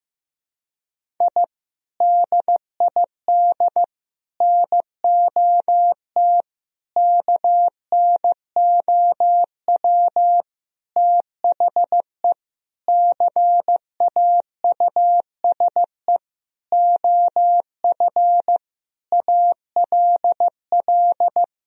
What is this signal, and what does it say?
Morse code, heard as a keyed tone, about 16 wpm, 700 Hz. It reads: I DID NOT KNOW THE CAUSE OF ALL
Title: Sentences from Top 1000 Words - Review Part 1 - 15wpm